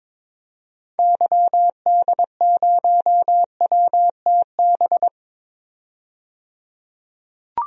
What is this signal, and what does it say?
Morse code, heard as a keyed tone, about 22 wpm, 700 Hz. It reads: YD0WTB
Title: DXCC-100 Call Signs - 6 Characters - 22wpm